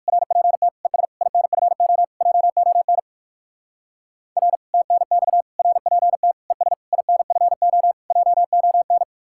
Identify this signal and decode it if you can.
Morse code, heard as a keyed tone, about 66 wpm, spaced as for 57 wpm, 700 Hz. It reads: RPT ES INFO JON R TNX RPT ES INFO JON